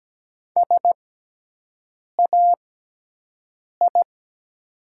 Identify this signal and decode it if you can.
Morse code, heard as a keyed tone, about 17 wpm, spaced as for 6 wpm, 700 Hz. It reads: SAI
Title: 6x Character Spacing - Top 100 Words - 17wpm